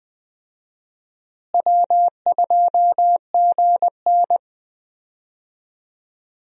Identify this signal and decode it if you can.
Morse code, heard as a keyed tone, about 20 wpm, 700 Hz. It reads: W2GN
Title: USA Call Signs - 4 Characters - Repeated 2x - 20wpm